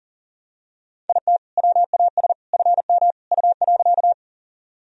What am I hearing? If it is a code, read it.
Morse code, heard as a keyed tone, about 40 wpm, 700 Hz. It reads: IT WAS FM U.
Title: Sentences from Top 100 Words and Abbreviations 40wpm